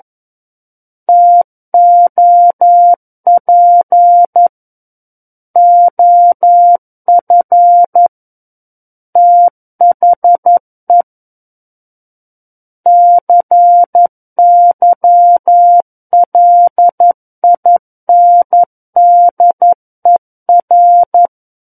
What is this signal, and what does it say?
Morse code, heard as a keyed tone, about 11 wpm, 700 Hz. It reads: TOP OF THE CYLINDER
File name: 11 WPM